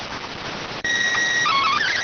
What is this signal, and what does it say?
SSTV (slow-scan television), PD 290.